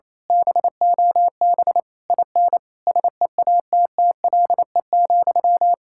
Morse code, heard as a keyed tone, about 26 wpm, 700 Hz. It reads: BOB IN SEATTLE,
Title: Speed-Racing Common QSO Phrases - 28wpm